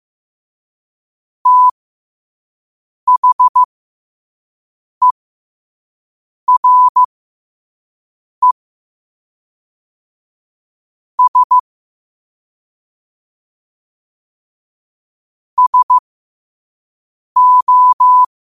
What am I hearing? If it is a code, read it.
Morse code, heard as a keyed tone, about 15 wpm, spaced as for 5 wpm, 1000 Hz. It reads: THERE S SO
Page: New quotes every day in morse code at 5 Words per minute.